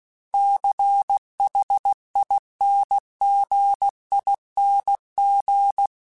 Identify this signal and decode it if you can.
Morse code, heard as a keyed tone, about 16 wpm, 800 Hz. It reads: CHINGING
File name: Special INGs-mp3s / Special INGs-mp3s / Dble Ings @15